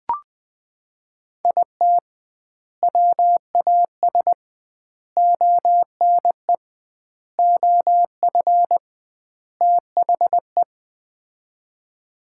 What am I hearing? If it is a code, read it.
Morse code, heard as a keyed tone, about 20 wpm, 700 Hz. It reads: IT WAS ONE OF THE
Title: 2x Word-Spacing - Sets of 5 Words - 20wpm